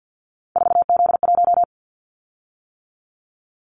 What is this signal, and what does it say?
Morse code, heard as a keyed tone, about 50 wpm, 700 Hz. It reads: 4Z1
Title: Callsign Prefixes for Top 50 DXCC - 50wpm